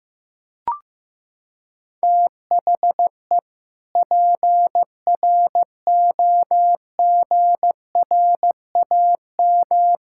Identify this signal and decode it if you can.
Morse code, heard as a keyed tone, about 15 wpm, 700 Hz. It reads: THE PROGRAM
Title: Sentences from Top 1000 Words - Review Part 5 - 15wpm